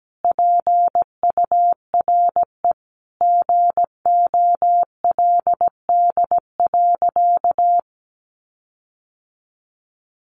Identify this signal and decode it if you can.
Morse code, heard as a keyed tone, about 17 wpm, 700 Hz. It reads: PURE GOLD.
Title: Sentences from Top 1500 Words 17wpm